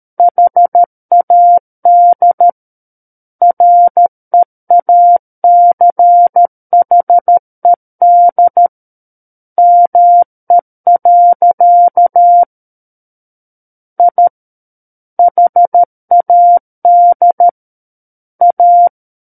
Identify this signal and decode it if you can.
Morse code, heard as a keyed tone, about 13 wpm, 700 Hz. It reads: HAD REACHED ME. I HAD A
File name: Book One / 04-Chapter 4 - 13 WPM